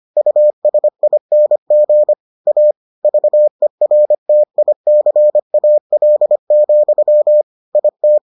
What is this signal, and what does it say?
Morse code, heard as a keyed tone, about 25 wpm, 600 Hz. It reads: USING A VERTICAL, IT